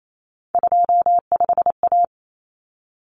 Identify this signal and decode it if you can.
Morse code, heard as a keyed tone, about 28 wpm, 700 Hz. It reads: 25A
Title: Common Cut Numbers - 3-Digit 28wpm